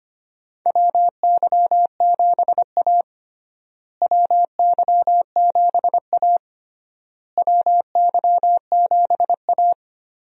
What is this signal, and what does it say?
Morse code, heard as a keyed tone, about 25 wpm, 700 Hz. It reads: WY7A WY7A WY7A
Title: USA Call Signs - 4 Characters - Repeated 3x - 25wpm